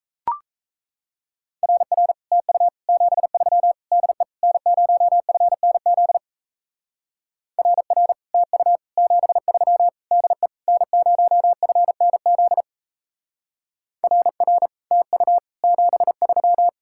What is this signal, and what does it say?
Morse code, heard as a keyed tone, about 33 wpm, 700 Hz. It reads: RR TU 73 DE N0FNZ RR TU 73 DE N0FNZ RR TU 73
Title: Speed-Racing Common QSO Phrases - 28wpm